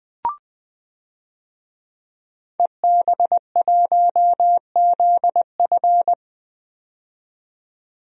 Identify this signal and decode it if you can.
Morse code, heard as a keyed tone, about 20 wpm, 700 Hz. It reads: EB1ZF
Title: DXCC-100 Call Signs - 5 Characters - 20wpm